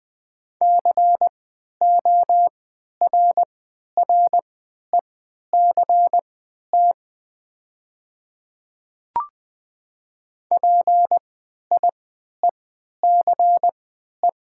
Morse code, heard as a keyed tone, about 20 wpm, spaced as for 11 wpm, 700 Hz. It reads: CORRECT PIECE
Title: 3x Character Spacing - Top 500 Words - 20wpm